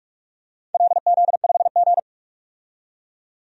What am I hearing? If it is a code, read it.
Morse code, heard as a keyed tone, about 43 wpm, 700 Hz. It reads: LZ5G
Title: DXCC-100 Call Signs - 4 Characters - 45wpm